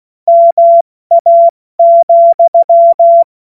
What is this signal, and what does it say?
Morse code, heard as a keyed tone, about 16 wpm, 680 Hz. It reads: MA,
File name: Morsekurs / Witze / Witz 24. Januar 2023 70 F80